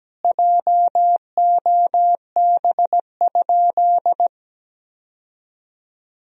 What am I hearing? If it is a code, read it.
Morse code, heard as a keyed tone, about 17 wpm, 700 Hz. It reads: JOB?
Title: Sentences from Top 1000 Words - Review Part 3 - 17wpm